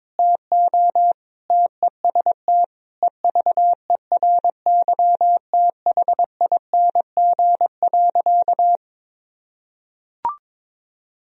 Morse code, heard as a keyed tone, about 22 wpm, 700 Hz. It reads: TO TEST EVERYTHING.
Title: Sentences from Top 1000 Words - Review Part 3 - 22wpm